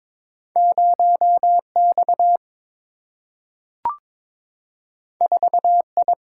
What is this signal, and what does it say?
Morse code, heard as a keyed tone, about 22 wpm, 700 Hz. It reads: 0X 4I
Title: Single-Letter Single-Number 22wpm